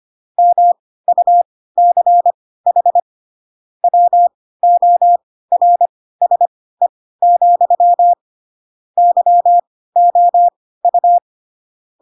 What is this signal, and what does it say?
Morse code, heard as a keyed tone, about 25 wpm, spaced as for 16 wpm, 700 Hz. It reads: MUCH WORSE, YOU
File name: Pinocchio / Pinocchio ch4 10 25 ik1hge